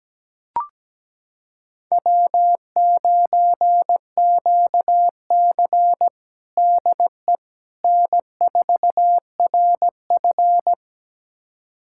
Morse code, heard as a keyed tone, about 17 wpm, 700 Hz. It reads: W9QC DE N4RF
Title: USA Callsign DE Callsign - 4 Characters - 17wpm